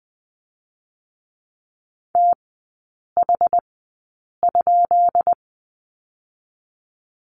Morse code, heard as a keyed tone, about 20 wpm, spaced as for 8 wpm, 700 Hz. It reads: TH?
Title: 159 Lesson 26e - Letter g - Three Characters 20wpm